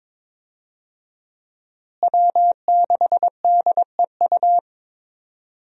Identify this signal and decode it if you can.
Morse code, heard as a keyed tone, about 22 wpm, 700 Hz. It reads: W6DEU
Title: USA Call Signs - 5 Characters - Repeated 3x - 22wpm